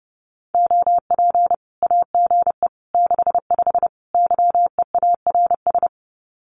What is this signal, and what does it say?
Morse code, heard as a keyed tone, about 29 wpm, 700 Hz. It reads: OP AGE 65 YEARS
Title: Common QSO Phrases 30wpm